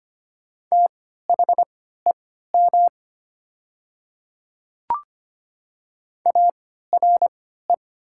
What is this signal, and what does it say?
Morse code, heard as a keyed tone, about 25 wpm, spaced as for 14 wpm, 700 Hz. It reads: THEM ARE
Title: 3x Character Spacing - Top 100 Words - 25wpm